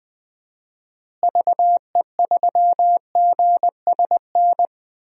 Morse code, heard as a keyed tone, about 20 wpm, 700 Hz. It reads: VE3GSN
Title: DXCC-100 Call Signs - 6 Characters - Repeated 2x - 20wpm